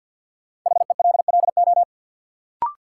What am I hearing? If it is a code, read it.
Morse code, heard as a keyed tone, about 51 wpm, 700 Hz. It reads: HELLO
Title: Sentences from Top 1100 Words - New Word Drill - 50wpm